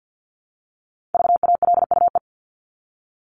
Morse code, heard as a keyed tone, about 50 wpm, 700 Hz. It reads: VALUE
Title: Top 501-750 Words 50wpm